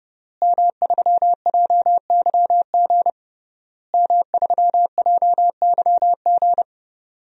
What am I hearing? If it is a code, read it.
Morse code, heard as a keyed tone, about 32 wpm, 700 Hz. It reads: M3JYG M3JYG
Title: DXCC-100 Call Signs - 5 Characters - Repeated 3x - 30wpm